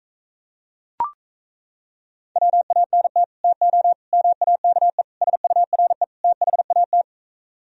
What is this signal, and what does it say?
Morse code, heard as a keyed tone, about 38 wpm, 700 Hz. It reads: WANT TO MAKE SURE THAT